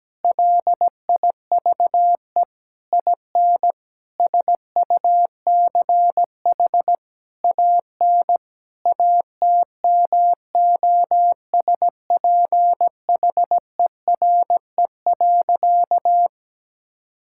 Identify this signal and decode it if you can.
Morse code, heard as a keyed tone, about 17 wpm, 700 Hz. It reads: LIVE IN SUCH AN ATMOSPHERE.
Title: Sentences from Top 1600 Words 17wpm